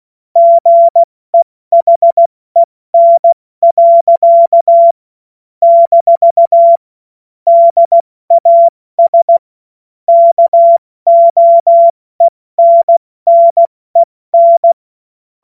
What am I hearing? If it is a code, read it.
Morse code, heard as a keyed tone, about 16 wpm, 680 Hz. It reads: GEHEN. - DAS KOENNEN
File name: Morsekurs / Witze / Witz 04. April 70 F80